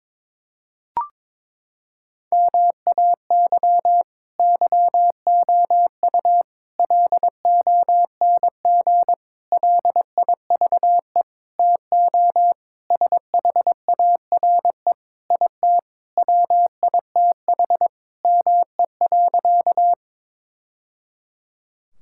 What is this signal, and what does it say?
Morse code, heard as a keyed tone, about 22 wpm, 700 Hz. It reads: MAY YOU LONG LIVE TO SHARE IT WITH ME.
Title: Sentences from Top 1000 Words - Review Part 6 - 22wpm